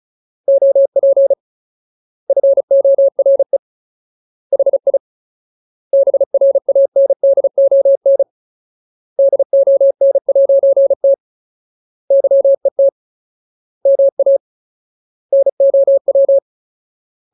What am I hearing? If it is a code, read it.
Morse code, heard as a keyed tone, about 34 wpm, 550 Hz. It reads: OP FORE HI BRANDON DON'T YET MA NOW
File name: Ultimate Word List / 35WPM Ultimate